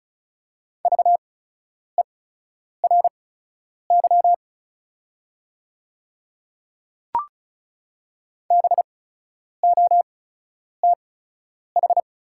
Morse code, heard as a keyed tone, about 36 wpm, spaced as for 10 wpm, 700 Hz. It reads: VERY BOTH
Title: 8x Character Spacing - Top 300 Words - 35wpm